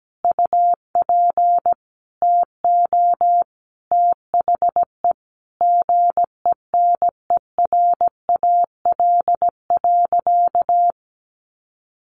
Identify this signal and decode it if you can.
Morse code, heard as a keyed tone, about 17 wpm, 700 Hz. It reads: UP TO THE GENERAL.
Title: Sentences from Top 1500 Words 17wpm